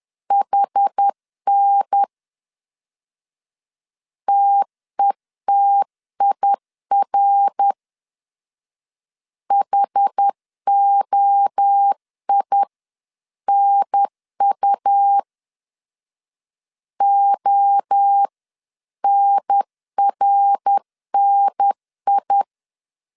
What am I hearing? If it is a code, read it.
Morse code, heard as a keyed tone, about 10 wpm, 800 Hz. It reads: HN TETIR HOI NU ONRNI
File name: 10wpm